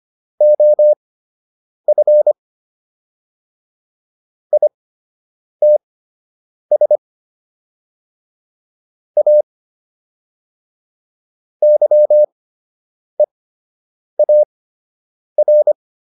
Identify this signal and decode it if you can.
Morse code, heard as a keyed tone, about 25 wpm, spaced as for 8 wpm, 600 Hz. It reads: OF ITS A YEAR